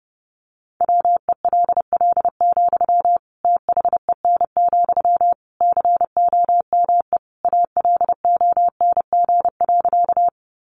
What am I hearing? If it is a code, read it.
Morse code, heard as a keyed tone, about 28 wpm, 700 Hz. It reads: WELL, THEN, COME ALONG.
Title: Sentences from Top 700 Words 30wpm